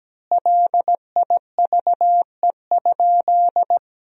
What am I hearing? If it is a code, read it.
Morse code, heard as a keyed tone, about 17 wpm, 700 Hz. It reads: LIVE?
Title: 2x Word-Spacing - Sentences from Top 200 Words - 17wpm